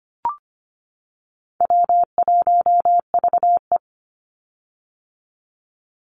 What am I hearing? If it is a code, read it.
Morse code, heard as a keyed tone, about 25 wpm, 700 Hz. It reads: W1VE
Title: Call Signs - 25wpm